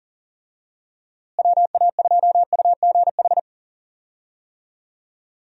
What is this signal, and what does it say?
Morse code, heard as a keyed tone, about 40 wpm, 700 Hz. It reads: WA2UGH
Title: USA Call Signs - 6 Characters - Repeated 2x - 40wpm